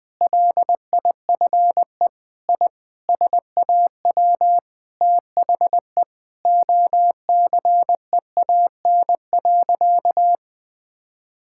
Morse code, heard as a keyed tone, about 21 wpm, 700 Hz. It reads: LIFE I SAW THE OCEAN.
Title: Sentences with 8-Word Phrases - 20wpm